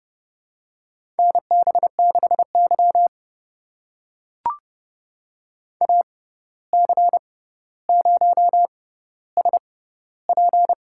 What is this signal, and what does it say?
Morse code, heard as a keyed tone, about 30 wpm, spaced as for 14 wpm, 700 Hz. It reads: NB6Y AC0SP
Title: USA Call Signs - 6x Character Spacing - 30wpm